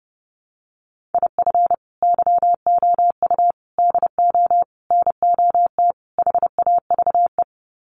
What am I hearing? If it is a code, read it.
Morse code, heard as a keyed tone, about 30 wpm, 700 Hz. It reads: IF YOU DO NOT HAVE
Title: Sets of 5 Words 30wpm